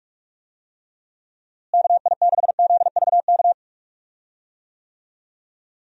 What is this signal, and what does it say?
Morse code, heard as a keyed tone, about 46 wpm, 700 Hz. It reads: KI6ZVK
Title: USA Call Signs - 6 Characters - Repeated 3x - 45wpm